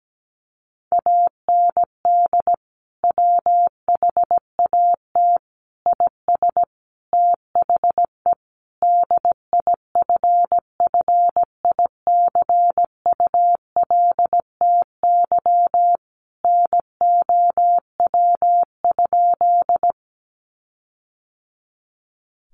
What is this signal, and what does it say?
Morse code, heard as a keyed tone, about 17 wpm, 700 Hz. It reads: AND WHAT IS THE DIFFICULTY NOW?
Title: Sentences from Top 1500 Words 17wpm